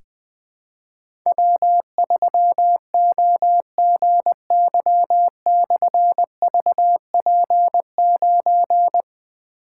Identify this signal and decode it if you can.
Morse code, heard as a keyed tone, about 20 wpm, 700 Hz. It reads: W3OGY/VP9
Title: Difficult Call Signs 20wpm